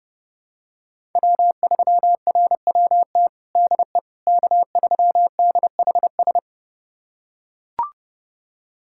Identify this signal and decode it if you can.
Morse code, heard as a keyed tone, about 32 wpm, 700 Hz. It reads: W3RWT DE K3DHS
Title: USA Callsign DE Callsign - 5 Characters - 30wpm